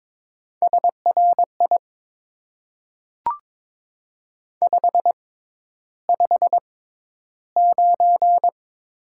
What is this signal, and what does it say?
Morse code, heard as a keyed tone, about 22 wpm, spaced as for 11 wpm, 700 Hz. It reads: SRI 559
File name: Top 90 QSO Elements - 6x Character Spacing - 22wpm